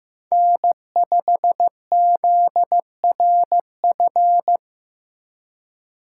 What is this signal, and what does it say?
Morse code, heard as a keyed tone, about 18 wpm, 700 Hz. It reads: N5ZRF
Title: Speed Racing - USA 1x3 Call Signs 15wpm